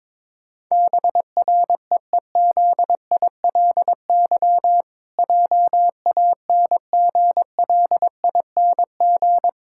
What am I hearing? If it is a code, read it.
Morse code, heard as a keyed tone, about 22 wpm, 700 Hz. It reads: BREEZILY JANGLING
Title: Pangram Challenge v2 - 22wpm